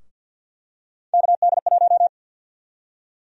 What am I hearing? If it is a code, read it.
Morse code, heard as a keyed tone, about 52 wpm, 700 Hz. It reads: KD1
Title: USA Call Sign Prefixes - Repeated 2x - 50wpm